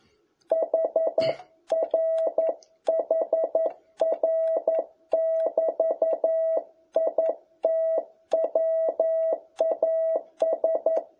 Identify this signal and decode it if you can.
Morse code, heard as a keyed tone, about 11 wpm, 660 Hz. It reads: HR HR<BT>IT WAS
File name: jh8cbh / 2021cw